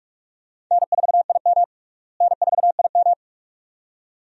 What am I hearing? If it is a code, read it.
Morse code, heard as a keyed tone, about 42 wpm, 700 Hz. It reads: N4IM N4IM
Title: USA Call Signs - 4 Characters - Repeated 2x - 45wpm